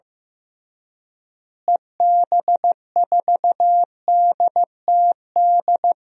Morse code, heard as a keyed tone, about 15 wpm, 700 Hz. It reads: EB4DTD
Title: DXCC-100 Call Signs - 6 Characters - Repeated 3x - 15wpm